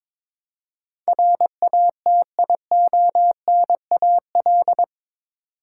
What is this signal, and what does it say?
Morse code, heard as a keyed tone, about 22 wpm, 700 Hz. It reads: RATIONAL
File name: practice / 22wpm / Eight Letter Words - 22wpm